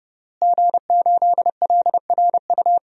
Morse code, heard as a keyed tone, about 30 wpm, 700 Hz. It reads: G8LRU
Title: DXCC-100 Call Signs - 5 Characters - Repeated 3x - 30wpm